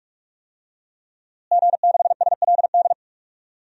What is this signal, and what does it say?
Morse code, heard as a keyed tone, about 46 wpm, 700 Hz. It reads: G6SLD
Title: DXCC-100 Call Signs - 5 Characters - Repeated 3x - 45wpm